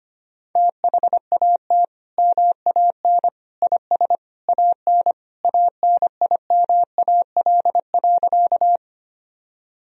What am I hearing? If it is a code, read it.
Morse code, heard as a keyed tone, about 25 wpm, 700 Hz. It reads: THAT MAN IS AN ANIMAL.
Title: Sentences from Top 1000 Words - Review Part 1 - 25wpm